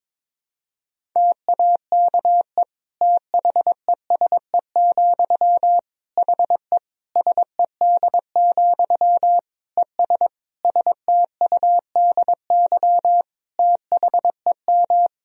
Morse code, heard as a keyed tone, about 22 wpm, 700 Hz. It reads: TAKE THESE, HE SED, ES STUDY THEM
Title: Sentences from Top 300 Words and Abbreviations 22wpm